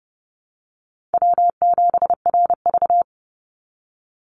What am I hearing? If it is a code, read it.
Morse code, heard as a keyed tone, about 30 wpm, 700 Hz. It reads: W7RV
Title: NA SOTA Chaser Callsigns 30wpm